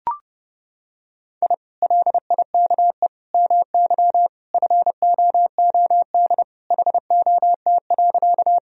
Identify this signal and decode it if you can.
Morse code, heard as a keyed tone, about 30 wpm, 700 Hz. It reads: I LIKE MY FOOD HOT.
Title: Sentences from Top 1000 Words - Review Part 3 - 30wpm